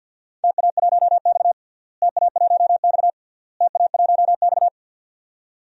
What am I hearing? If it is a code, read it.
Morse code, heard as a keyed tone, about 52 wpm, 700 Hz. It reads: TA1X TA1X TA1X
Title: DXCC-100 Call Signs - 4 Characters - Repeated 3x - 50wpm